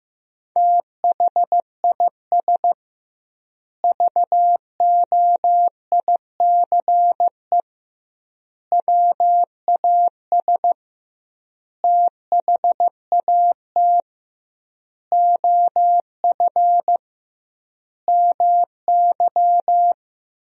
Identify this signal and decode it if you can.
Morse code, heard as a keyed tone, about 15 wpm, 700 Hz. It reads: THIS VOICE WAS THAT OF MY
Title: 2x Word-Spacing - Sentences from Top 500 Words - 15wpm